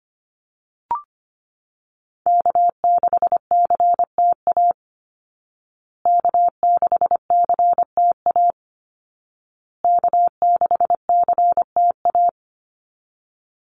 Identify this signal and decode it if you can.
Morse code, heard as a keyed tone, about 25 wpm, 700 Hz. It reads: K6CTA K6CTA K6CTA
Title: Call Signs Repeated 2x - 25wpm